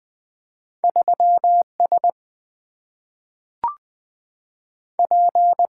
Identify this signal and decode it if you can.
Morse code, heard as a keyed tone, about 20 wpm, 700 Hz. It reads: 3S P
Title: Single-Letter Single-Number 20wpm